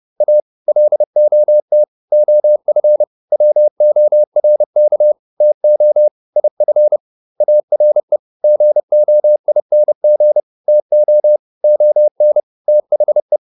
Morse code, heard as a keyed tone, about 30 wpm, 600 Hz. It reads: A LOT OF WORK TO IF ARE GOING TO ON THE